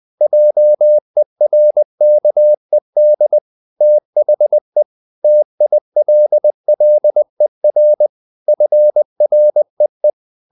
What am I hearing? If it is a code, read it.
Morse code, heard as a keyed tone, about 20 wpm, 600 Hz. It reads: JERKED THE TILLER FREE
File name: the old man and the sea part 2 20wpm